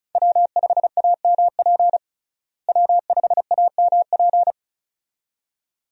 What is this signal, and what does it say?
Morse code, heard as a keyed tone, about 36 wpm, 700 Hz. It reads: W5AMP W5AMP
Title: USA Call Signs - 5 Characters - Repeated 2x - 35wpm